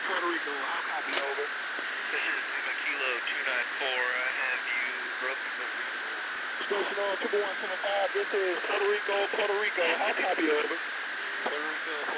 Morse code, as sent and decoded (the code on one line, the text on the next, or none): none